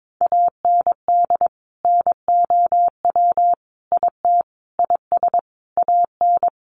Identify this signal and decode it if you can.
Morse code, heard as a keyed tone, about 22 wpm, 700 Hz. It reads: AND NOW IT IS AN
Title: Sentences from Top 1000 Words 22wpm